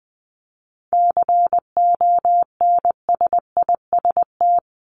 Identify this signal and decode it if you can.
Morse code, heard as a keyed tone, about 20 wpm, 700 Hz. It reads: CONSIST
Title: Top 1251-1500 Words 20wpm